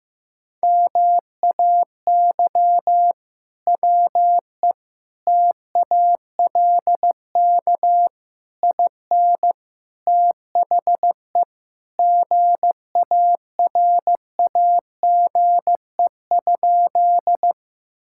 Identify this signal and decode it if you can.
Morse code, heard as a keyed tone, about 16 wpm, 700 Hz. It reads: MAY WE TALK IN THE GARAGE?
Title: Sentences from Top 1600 Words 15wpm